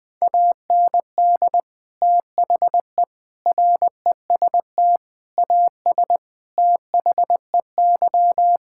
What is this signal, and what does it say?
Morse code, heard as a keyed tone, about 21 wpm, 700 Hz. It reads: AND THE REST AS THEY
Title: Sentences with 8-Word Phrases - 20wpm